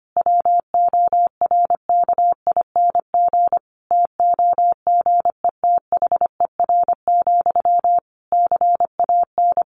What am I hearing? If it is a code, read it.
Morse code, heard as a keyed tone, about 25 wpm, 700 Hz. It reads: WORKING TOGETHER, CAN
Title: Sentences from Top 1400 Words 25wpm